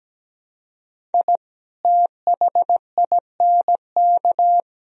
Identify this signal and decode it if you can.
Morse code, heard as a keyed tone, about 17 wpm, 700 Hz. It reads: I THINK
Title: Speed-Racing - Sets of 5 Words 17wpm